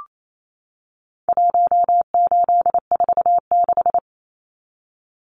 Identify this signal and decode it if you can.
Morse code, heard as a keyed tone, about 28 wpm, 700 Hz. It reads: 1846
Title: Numbers - Contest Style - 4-Digit 28wpm